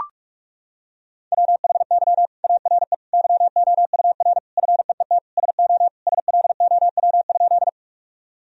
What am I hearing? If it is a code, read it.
Morse code, heard as a keyed tone, about 41 wpm, 700 Hz. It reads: WHY ARE YOUR FEET SO SLOW?
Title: Sentences from Top 1000 Words - Review Part 2 - 45wpm